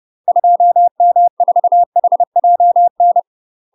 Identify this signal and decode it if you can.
Morse code, heard as a keyed tone, about 30 wpm, 700 Hz. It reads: 2M4HJN
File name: UK MASTER QSOs TRAINING / 30WPM / UK MASTER QSOs 0013